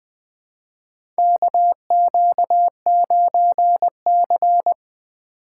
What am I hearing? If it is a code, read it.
Morse code, heard as a keyed tone, about 20 wpm, 700 Hz. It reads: KQ9C
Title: USA Call Signs - 4 Characters - 2x Character Spacing - 20wpm